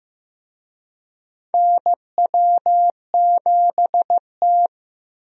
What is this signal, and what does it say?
Morse code, heard as a keyed tone, about 15 wpm, 700 Hz. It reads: NW7T
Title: USA Call Signs - 4 Characters - 15wpm